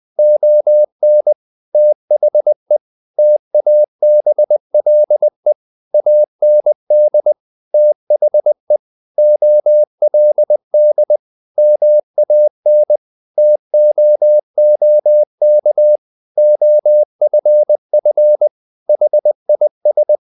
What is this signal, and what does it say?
Morse code, heard as a keyed tone, about 20 wpm, 600 Hz. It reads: ON THE TABLE AND THE OLD MAN TOOK OFF HIS
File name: the old man and the sea / the old man and the sea part 1 20wpm